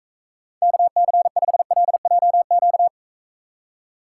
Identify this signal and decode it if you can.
Morse code, heard as a keyed tone, about 43 wpm, 700 Hz. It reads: KC5LJQ